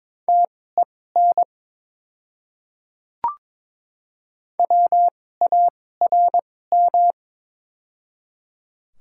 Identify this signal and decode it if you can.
Morse code, heard as a keyed tone, about 22 wpm, spaced as for 16 wpm, 700 Hz. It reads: TEN WARM
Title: 2x Character Spacing - Top 500 Words - 22wpm